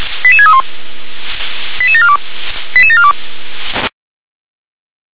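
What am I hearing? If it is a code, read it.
Five-tone selective call (ZVEI), heard as quick runs of five tones, tones of about 70 ms. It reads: 88631 88742 88742
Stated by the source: ZVEI five-tone selective call received on an SDR.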